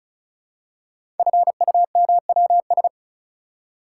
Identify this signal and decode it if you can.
Morse code, heard as a keyed tone, about 36 wpm, 700 Hz. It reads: FUMWS
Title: Code Groups - 5 Letters - 35wpm